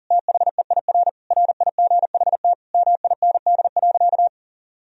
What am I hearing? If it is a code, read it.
Morse code, heard as a keyed tone, about 40 wpm, 700 Hz. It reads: THEIR RIGHT MIND.
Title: Sentences from Top 1000 Words - Review Part 1 - 40wpm